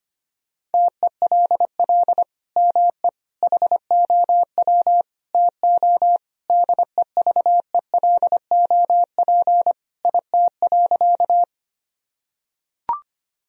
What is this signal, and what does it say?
Morse code, heard as a keyed tone, about 25 wpm, 700 Hz. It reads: TELL ME HOW TO DEVELOP IT.
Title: Sentences from Top 1000 Words - Review Part 2 - 25wpm